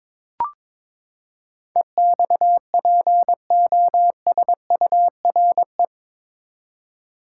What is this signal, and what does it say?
Morse code, heard as a keyed tone, about 22 wpm, 700 Hz. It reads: EXPOSURE
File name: practice / 22wpm / Eight Letter Words - 22wpm